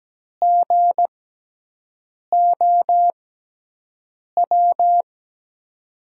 Morse code, heard as a keyed tone, about 17 wpm, spaced as for 6 wpm, 700 Hz. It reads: GOW
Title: USA Call Signs - 6 Characters - 6x Character Spacing - 17wpm